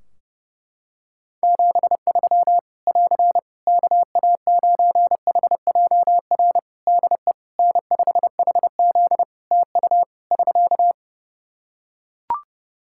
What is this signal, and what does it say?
Morse code, heard as a keyed tone, about 30 wpm, 700 Hz. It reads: 73 <AR> KA9HJR DE N5HZ TU <SK>
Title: Practice QSO - Exchange 4 - Ending - 30wpm